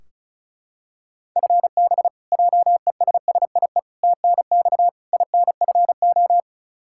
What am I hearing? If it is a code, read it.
Morse code, heard as a keyed tone, about 36 wpm, 700 Hz. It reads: FB JESSIE TNX INFO
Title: Practice QSO - Exchange 3 - More Info - 35wpm